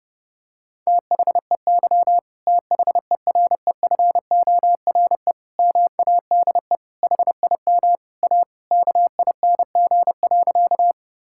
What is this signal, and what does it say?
Morse code, heard as a keyed tone, about 30 wpm, 700 Hz. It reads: THEY THEREFORE MADE HIM A KING.
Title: Sentences from Top 1000 Words - Review Part 5 - 30wpm